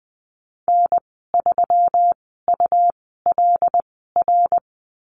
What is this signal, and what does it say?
Morse code, heard as a keyed tone, about 20 wpm, spaced as for 15 wpm, 700 Hz. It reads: N3ULR
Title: 186 Lesson 30g - Letter b - Call Signs 20wpm